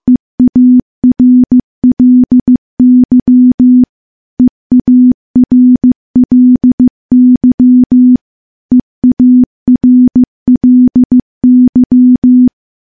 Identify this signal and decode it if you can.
Morse code, heard as a keyed tone, about 15 wpm, 250 Hz. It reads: EARLY EARLY EARLY
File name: morse-training-15WPM